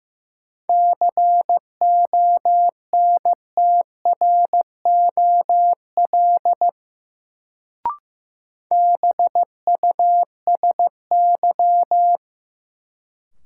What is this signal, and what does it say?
Morse code, heard as a keyed tone, about 15 wpm, 700 Hz. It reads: CONTROL BUSY
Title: Sentences from Top 1000 Words - Review Part 2 - New Word Drill - 15wpm